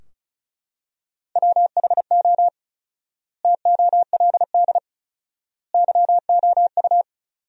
Morse code, heard as a keyed tone, about 34 wpm, 700 Hz. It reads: WHO TOLD YOU
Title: 4x Word-Spacing - Sentences from Top 400 Words - 35wpm